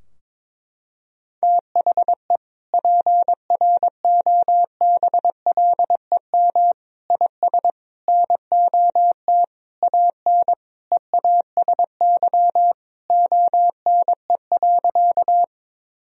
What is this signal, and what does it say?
Morse code, heard as a keyed tone, about 22 wpm, 700 Hz. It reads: THE PROBLEM IS NOT AN EASY ONE.
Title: Sentences from Top 1000 Words - Review Part 2 - 22wpm